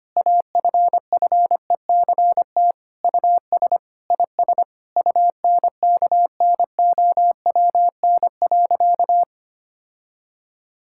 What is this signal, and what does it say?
Morse code, heard as a keyed tone, about 25 wpm, 700 Hz. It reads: AFFECT US IS UNKNOWN.
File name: practice / 25wpm / Sentences with 4-Word Phrases - 25wpm